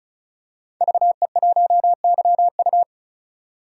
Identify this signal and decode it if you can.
Morse code, heard as a keyed tone, about 35 wpm, 700 Hz. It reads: VE1YU
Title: DXCC-100 Call Signs - 5 Characters - 35wpm